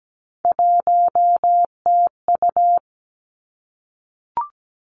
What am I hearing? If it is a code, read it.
Morse code, heard as a keyed tone, about 17 wpm, 700 Hz. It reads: 1TU
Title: Three Challenging Letter-Number Combinations 17wpm